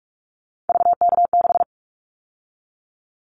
Morse code, heard as a keyed tone, about 44 wpm, 700 Hz. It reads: VK6
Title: Callsign Prefixes for Top 100 DXCC - 45wpm